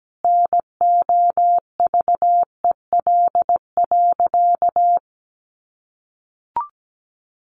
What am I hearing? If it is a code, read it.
Morse code, heard as a keyed tone, about 17 wpm, 700 Hz. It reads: NOVEL.
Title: Sentences from Top 1300 Words 17wpm